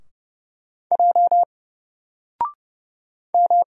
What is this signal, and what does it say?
Morse code, heard as a keyed tone, about 31 wpm, 700 Hz. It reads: J M
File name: Single-Letter-Number-30wpm